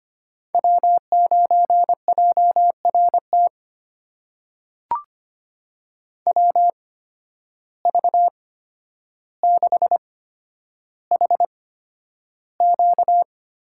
Morse code, heard as a keyed tone, about 25 wpm, spaced as for 11 wpm, 700 Hz. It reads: W9JRT WV6HQ
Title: USA Call Signs - 5 Characters - 8x Character Spacing - 25wpm